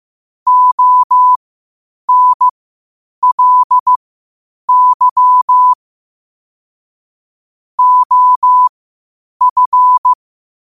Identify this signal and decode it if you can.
Morse code, heard as a keyed tone, about 15 wpm, spaced as for 8 wpm, 1000 Hz. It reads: ONLY OF
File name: qotdfiles / 8wpm